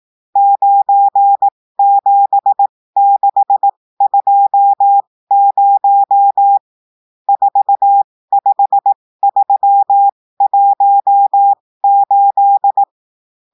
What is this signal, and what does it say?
Morse code, heard as a keyed tone, about 18 wpm, spaced as for 15 wpm, 800 Hz. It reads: 97620 45318
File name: cours-du-lundi files / cours-1580-texte-chiffre-15-mpm